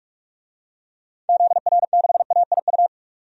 Morse code, heard as a keyed tone, about 41 wpm, 700 Hz. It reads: ZR6AIU
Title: DXCC-100 Call Signs - 6 Characters - 45wpm